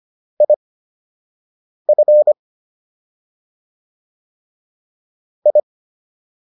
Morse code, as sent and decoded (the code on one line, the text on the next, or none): .. ..-. / ..
IF I